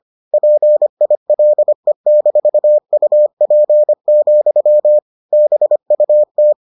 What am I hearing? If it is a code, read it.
Morse code, heard as a keyed tone, about 25 wpm, 600 Hz. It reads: PILE-UP, BUT